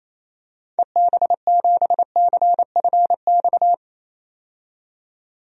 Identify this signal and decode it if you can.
Morse code, heard as a keyed tone, about 28 wpm, 700 Hz. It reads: EB7CFX
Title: DXCC-100 Call Signs - 6 Characters - 28wpm